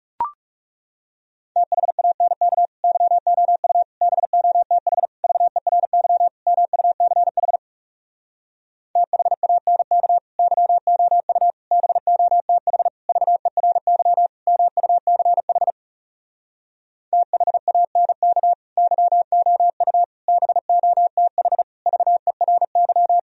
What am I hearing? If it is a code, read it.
Morse code, heard as a keyed tone, about 37 wpm, 700 Hz. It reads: THANK YOU BOTH VERY MUCH THANK YOU BOTH VERY MUCH THANK YOU BOTH VERY
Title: Speed-Racing - Sets of 5 Words 30wpm